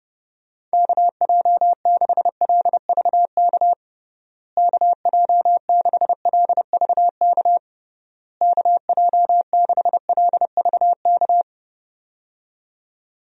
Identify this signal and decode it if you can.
Morse code, heard as a keyed tone, about 30 wpm, 700 Hz. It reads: KJ6LVK KJ6LVK KJ6LVK
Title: USA Call Signs - 6 Characters - Repeated 3x - 30wpm